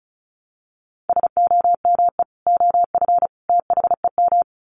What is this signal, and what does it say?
Morse code, heard as a keyed tone, about 34 wpm, 700 Hz. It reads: SOME OF THEM
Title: Sets of 3 Words 35wpm